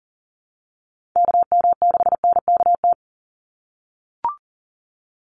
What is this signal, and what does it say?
Morse code, heard as a keyed tone, about 40 wpm, 700 Hz. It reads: KM6NKT
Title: USA Call Signs 40wpm